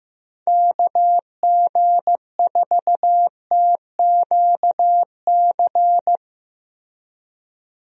Morse code, heard as a keyed tone, about 15 wpm, 700 Hz. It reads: KG4TQC
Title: USA Call Signs - 6 Characters - Repeated 2x - 15wpm